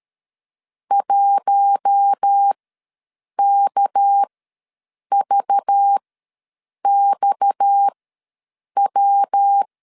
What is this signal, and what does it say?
Morse code, heard as a keyed tone, about 13 wpm, spaced as for 7 wpm, 800 Hz. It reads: 1KVXW
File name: LEZIONI / 7 GF9Z / 7 GF ws 12wpm 3140